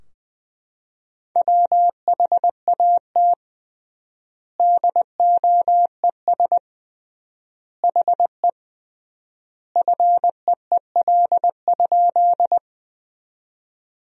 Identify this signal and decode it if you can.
Morse code, heard as a keyed tone, about 20 wpm, 700 Hz. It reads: WHAT DOES HE FEEL?
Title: 3x Word-Spacing - Sentences from Top 400 Words - 20wpm